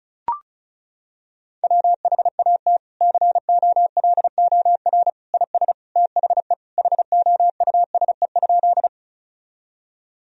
Morse code, heard as a keyed tone, about 36 wpm, 700 Hz. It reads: WHAT COLOR IS THE HOUSE?
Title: Sentences from Top 1000 Words - Review Part 2 - 35wpm